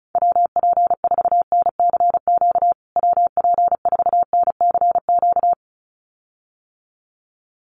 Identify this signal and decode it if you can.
Morse code, heard as a keyed tone, about 34 wpm, 700 Hz. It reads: WP4NCQ WP4NCQ
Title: USA Call Signs Repeated 2x 35wpm